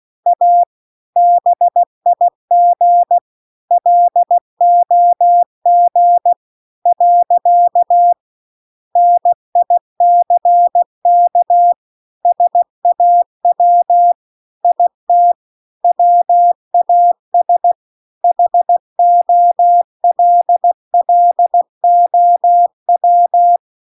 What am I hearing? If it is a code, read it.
Morse code, heard as a keyed tone, about 16 wpm, 700 Hz. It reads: A BIG LOG . NICK SAW IT WAS HOLLOW